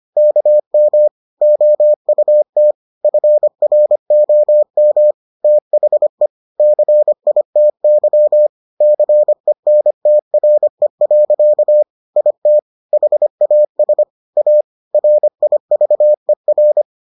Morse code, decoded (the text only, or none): KM OUT FROM THE CITY CENTRE. IT HAS A RIVER